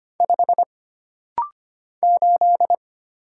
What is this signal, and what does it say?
Morse code, heard as a keyed tone, about 25 wpm, 700 Hz. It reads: 5 8
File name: Single Character (Letter:Number:Punctuation:Prosign) - Rapid-Fire - 25wpm